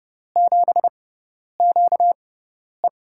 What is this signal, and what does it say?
Morse code, heard as a keyed tone, about 30 wpm, spaced as for 10 wpm, 690 Hz. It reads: 7QE
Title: USA Call Signs - 4 Characters - 6x Character Spacing - 30wpm